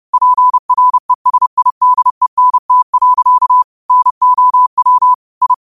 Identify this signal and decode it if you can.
Morse code, heard as a keyed tone, about 30 wpm, 1000 Hz. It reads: PRESIDENT. NOW I
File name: qotdfiles / 30wpm